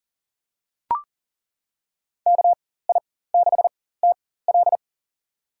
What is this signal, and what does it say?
Morse code, heard as a keyed tone, about 40 wpm, spaced as for 19 wpm, 700 Hz. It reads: KI6TL
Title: USA Call Signs - 5 Characters - 4x Character Spacing - 40wpm